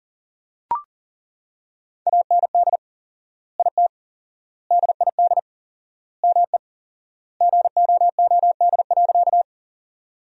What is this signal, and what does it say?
Morse code, heard as a keyed tone, about 37 wpm, 700 Hz. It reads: AND IT DID ME GOOD.
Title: 4x Word-Spacing - Sentences from Top 200 Words - 40wpm